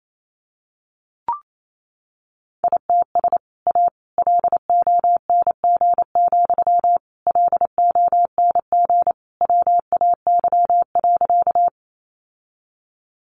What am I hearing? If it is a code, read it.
Morse code, heard as a keyed tone, about 28 wpm, 700 Hz. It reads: ITS A LONG, LONG WAY.
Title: Sentences from Top 600 Words 28wpm